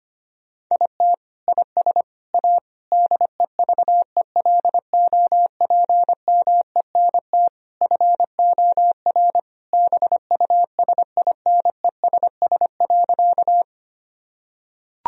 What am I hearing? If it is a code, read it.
Morse code, heard as a keyed tone, about 25 wpm, 700 Hz. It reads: IT IS A DEVELOPMENT FOR BUSINESS.
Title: Sentences from Top 1000 Words - Review Part 4 - 25wpm